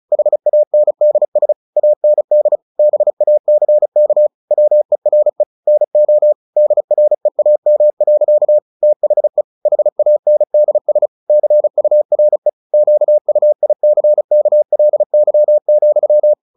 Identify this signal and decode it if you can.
Morse code, heard as a keyed tone, about 34 wpm, 600 Hz. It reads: HANDS AND BACK WERE NO DREAM. THE HANDS CURE QUICKLY,
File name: the old man and the sea 2 35wpm